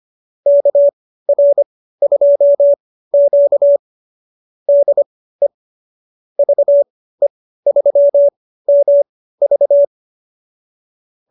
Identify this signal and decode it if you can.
Morse code, heard as a keyed tone, about 25 wpm, spaced as for 15 wpm, 570 Hz. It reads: KR2Q DE VE3MV